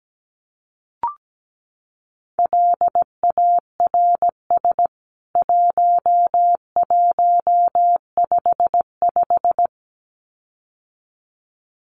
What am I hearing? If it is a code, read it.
Morse code, heard as a keyed tone, about 17 wpm, 700 Hz. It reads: LARS 1155
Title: CWT Exchange 17wpm